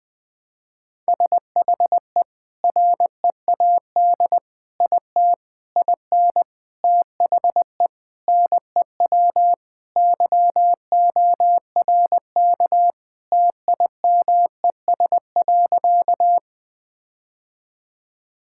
Morse code, heard as a keyed tone, about 20 wpm, 700 Hz. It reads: SHE READ IT IN THE NEW YORK TIMES.
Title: Sentences with 5-Word Phrases - 20wpm